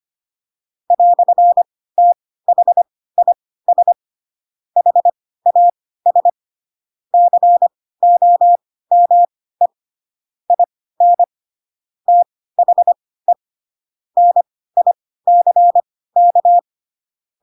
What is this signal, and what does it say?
Morse code, heard as a keyed tone, about 25 wpm, spaced as for 16 wpm, 700 Hz. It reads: "THIS HAS COME IN THE NICK
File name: Pinocchio / Pinocchio ch1 10 25 ik1hge